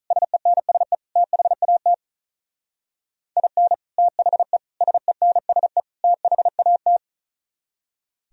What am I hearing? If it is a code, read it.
Morse code, heard as a keyed tone, about 44 wpm, 700 Hz. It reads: SENSE THAT IN THE SENSE THAT